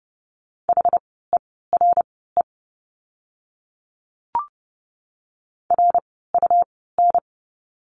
Morse code, heard as a keyed tone, about 30 wpm, spaced as for 17 wpm, 700 Hz. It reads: HERE RUN
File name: course / 30:10wpm / 115 Lesson 19f - Letter w - Words 30wpm